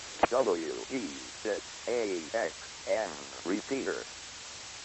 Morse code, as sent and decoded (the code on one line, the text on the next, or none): none